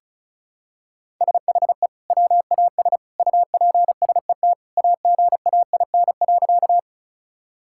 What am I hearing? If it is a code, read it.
Morse code, heard as a keyed tone, about 36 wpm, 700 Hz. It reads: SHE WAS UPSET AGAIN.
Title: Sentences from Top 1000 Words - Review Part 5 - 35wpm